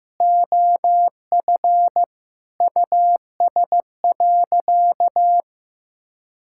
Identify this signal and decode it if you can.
Morse code, heard as a keyed tone, about 15 wpm, 700 Hz. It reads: OF US.
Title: Sentences from Top 1000 Words - Review Part 6 - 15wpm